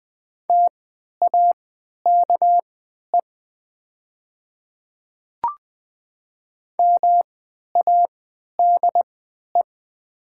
Morse code, heard as a keyed tone, about 20 wpm, spaced as for 11 wpm, 700 Hz. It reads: TAKE MADE
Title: 3x Character Spacing - Top 200 Words - 20wpm